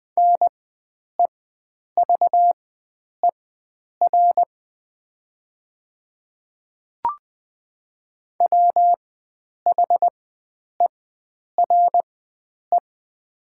Morse code, heard as a keyed tone, about 20 wpm, spaced as for 9 wpm, 700 Hz. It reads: NEVER WHERE
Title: 4x Character Spacing - Top 300 Words - 20wpm